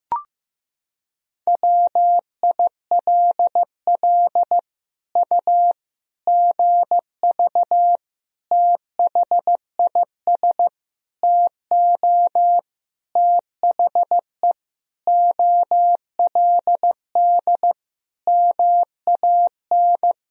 Morse code, heard as a keyed tone, about 15 wpm, 700 Hz. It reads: WILL U GV THIS TO THE OLD MAN
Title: Sentences from Top 200 Words and Abbreviations 15wpm